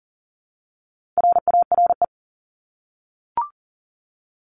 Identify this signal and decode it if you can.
Morse code, heard as a keyed tone, about 40 wpm, 700 Hz. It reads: RARE
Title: LY Suffix - Top 100 Words with Base Words 40wpm